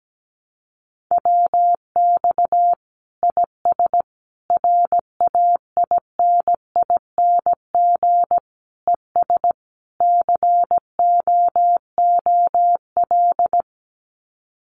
Morse code, heard as a keyed tone, about 17 wpm, 700 Hz. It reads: WX IS RAINING ES COOL
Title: Common QSO Phrases 17wpm